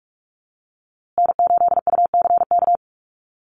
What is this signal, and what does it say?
Morse code, heard as a keyed tone, about 44 wpm, 700 Hz. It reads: N8UCK
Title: USA Call Signs Repeated 2x 45wpm